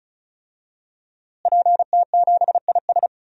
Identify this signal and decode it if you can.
Morse code, heard as a keyed tone, about 36 wpm, 700 Hz. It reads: PT7IS
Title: DXCC-100 Call Signs - 5 Characters - Repeated 2x - 35wpm